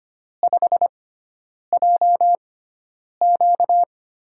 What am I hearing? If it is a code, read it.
Morse code, heard as a keyed tone, about 25 wpm, spaced as for 9 wpm, 700 Hz. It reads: 5JQ
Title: USA Call Signs - 4 Characters - 6x Character Spacing - 25wpm